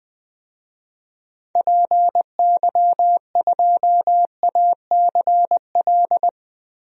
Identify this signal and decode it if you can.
Morse code, heard as a keyed tone, about 20 wpm, 700 Hz. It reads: PY2ACL
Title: DXCC-100 Call Signs - 6 Characters - 20wpm